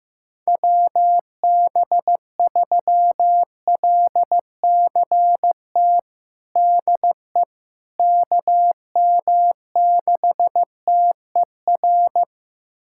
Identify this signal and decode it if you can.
Morse code, heard as a keyed tone, about 15 wpm, 700 Hz. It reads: WB3LCT DE KM6TER
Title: USA Callsign DE Callsign 15wpm